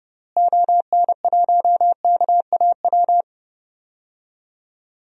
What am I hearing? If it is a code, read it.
Morse code, heard as a keyed tone, about 30 wpm, 700 Hz. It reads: ON1KAW
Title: DXCC-100 Call Signs - 6 Characters - Repeated 3x - 30wpm